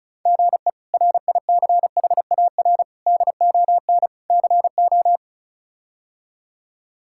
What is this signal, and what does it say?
Morse code, heard as a keyed tone, about 36 wpm, 700 Hz. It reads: GE RICHAR DON CO
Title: K1USN SST QSO Practice - 35wpm